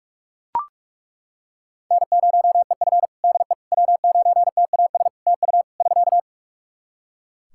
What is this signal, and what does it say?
Morse code, heard as a keyed tone, about 41 wpm, 700 Hz. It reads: N0EF DE W9TAS TU <SK>
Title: Practice QSO - Exchange 4 - Ending - 45wpm